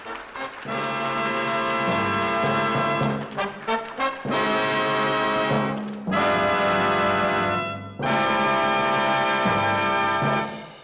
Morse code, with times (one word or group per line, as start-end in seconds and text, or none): none